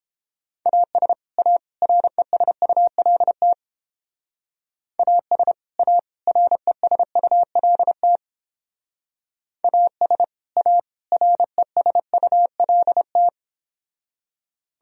Morse code, heard as a keyed tone, about 29 wpm, 700 Hz. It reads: AS A RESULT AS A RESULT AS A RESULT